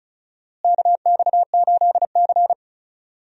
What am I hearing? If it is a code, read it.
Morse code, heard as a keyed tone, about 36 wpm, 700 Hz. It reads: KX8C
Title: USA Call Signs - 4 Characters - Repeated 2x - 35wpm